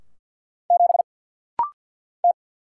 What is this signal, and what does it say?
Morse code, heard as a keyed tone, about 50 wpm, 690 Hz. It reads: 7 T
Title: Single Letter-Number - Rapid-Fire 50wpm